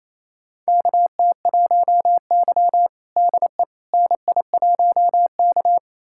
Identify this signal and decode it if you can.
Morse code, heard as a keyed tone, about 28 wpm, 700 Hz. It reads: KT1Y DE NI1K
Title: USA Callsign DE Callsign - 4 Characters - 28wpm